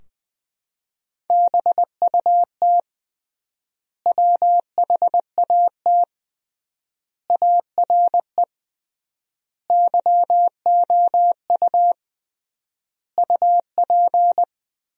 Morse code, heard as a keyed tone, about 20 wpm, 700 Hz. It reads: BUT WHAT ARE YOU UP
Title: 3x Word-Spacing - Sentences from Top 100 Words - 20wpm